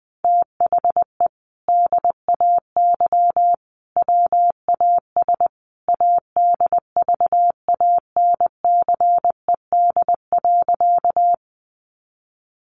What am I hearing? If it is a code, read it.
Morse code, heard as a keyed tone, about 20 wpm, 700 Hz. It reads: THE DAY WAS ADVANCED.
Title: Sentences from Top 1200 Words 20wpm